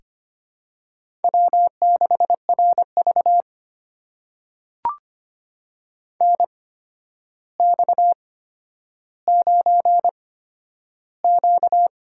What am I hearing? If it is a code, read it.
Morse code, heard as a keyed tone, about 25 wpm, spaced as for 11 wpm, 700 Hz. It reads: W6RV NX9Q
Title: USA Call Signs - 4 Characters - 8x Character Spacing - 25wpm